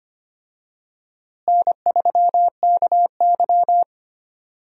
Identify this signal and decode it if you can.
Morse code, heard as a keyed tone, about 25 wpm, 700 Hz. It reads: N3KY
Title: USA Call Signs - 4 Characters - Repeated 2x - 25wpm